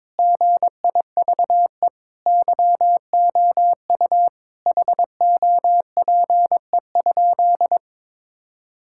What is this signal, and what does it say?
Morse code, heard as a keyed tone, about 22 wpm, 700 Hz. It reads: GIVE YOU HOPE?
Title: Sentences from Top 1000 Words - Review Part 5 - 22wpm